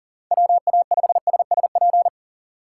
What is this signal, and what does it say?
Morse code, heard as a keyed tone, about 40 wpm, 700 Hz. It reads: WA5SSP
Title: USA Call Signs - 6 Characters - 4x Character Spacing - 40wpm